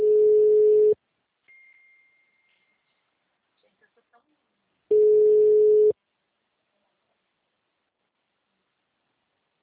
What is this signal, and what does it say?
Ringback tone (Europe, 425 Hz), with speech.